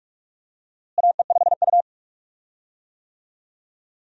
Morse code, heard as a keyed tone, about 44 wpm, 700 Hz. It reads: AE5U
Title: USA Call Signs - 4 Characters - Repeated 2x - 45wpm